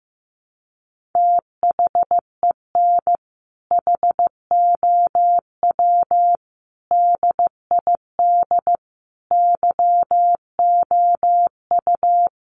Morse code, heard as a keyed tone, about 15 wpm, 700 Hz. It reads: THEN HOW DID YOU
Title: Sentences from Top 1400 Words 15wpm